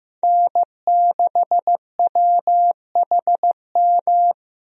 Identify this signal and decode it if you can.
Morse code, heard as a keyed tone, about 15 wpm, 700 Hz. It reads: N6WHM
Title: USA Call Signs - 5 Characters - Repeated 2x - 15wpm